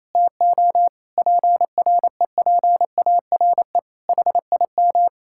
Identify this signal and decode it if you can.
Morse code, heard as a keyed tone, about 28 wpm, 700 Hz. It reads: TO PREPARE HIM
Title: Sentences from Top 1000 Words - Review Part 6 - 28wpm